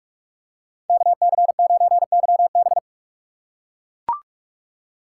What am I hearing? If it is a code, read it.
Morse code, heard as a keyed tone, about 40 wpm, 700 Hz. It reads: KC9YB